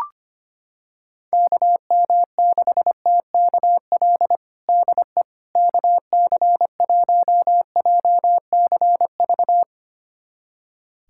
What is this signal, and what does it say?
Morse code, heard as a keyed tone, about 25 wpm, 700 Hz. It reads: KM6TKL DE KC1JCV
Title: USA Callsign DE Callsign 25wpm